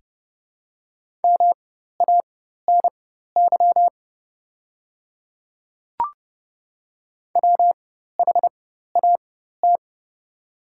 Morse code, heard as a keyed tone, about 30 wpm, spaced as for 14 wpm, 700 Hz. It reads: MANY WHAT
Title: 4x Character Spacing - Top 100 Words - 30wpm